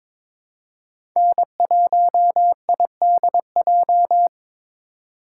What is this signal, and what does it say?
Morse code, heard as a keyed tone, about 22 wpm, 700 Hz. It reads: N1IDJ
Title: USA Call Signs - 3x Character Spacing - 22wpm